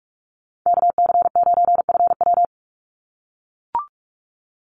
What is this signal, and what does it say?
Morse code, heard as a keyed tone, about 44 wpm, 700 Hz. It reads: KC9FW
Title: USA Call Signs 45wpm